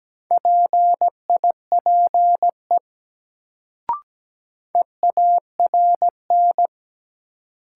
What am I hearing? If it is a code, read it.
Morse code, heard as a keyed tone, about 17 wpm, 700 Hz. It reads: PIPE EARN
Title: Sentences from Top 1500 Words - New Word Drill - 17wpm